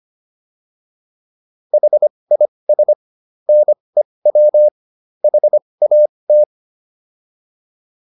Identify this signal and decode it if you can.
Morse code, heard as a keyed tone, about 25 wpm, spaced as for 20 wpm, 600 Hz. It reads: HIS NEW HAT